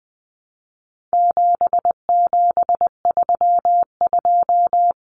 Morse code, heard as a keyed tone, about 20 wpm, 700 Hz. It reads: 7732
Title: Common Cut Numbers - 4-Digit 20wpm